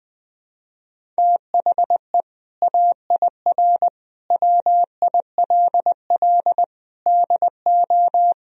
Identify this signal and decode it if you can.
Morse code, heard as a keyed tone, about 20 wpm, 700 Hz. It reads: THE AIR WILL DO
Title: Sentences from Top 1000 Words - Review Part 1 - 20wpm